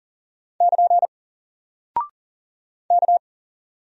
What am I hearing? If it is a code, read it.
Morse code, heard as a keyed tone, about 41 wpm, 700 Hz. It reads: <KN> K
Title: Single Character (Letter:Number:Punctuation:Prosign) 40wpm